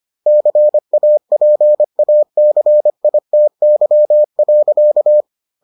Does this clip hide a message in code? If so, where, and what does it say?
Yes: Morse code from 0.3 to 5.2 s, heard as a keyed tone, about 25 wpm, 600 Hz. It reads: CAPACITY.